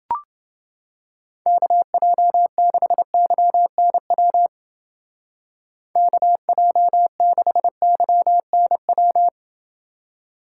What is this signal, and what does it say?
Morse code, heard as a keyed tone, about 30 wpm, 700 Hz. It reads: KJ6YNW KJ6YNW